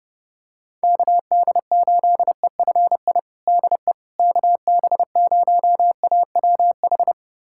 Morse code, heard as a keyed tone, about 32 wpm, 700 Hz. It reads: KD8EFI DE KB0AWH
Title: USA Callsign DE Callsign - 6 Characters - 30wpm